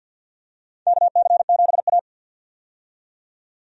Morse code, heard as a keyed tone, about 51 wpm, 700 Hz. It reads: KC7A
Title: USA Call Signs - 4 Characters - 50wpm